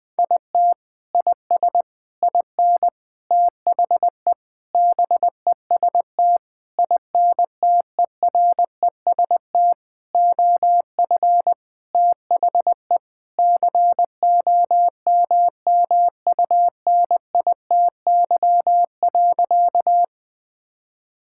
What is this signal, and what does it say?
Morse code, heard as a keyed tone, about 20 wpm, 700 Hz. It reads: IT IS IN THE BEST INTEREST OF THE COMMUNITY.
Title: Sentences with 6-Word Phrases - Spoken Beginning - 20wpm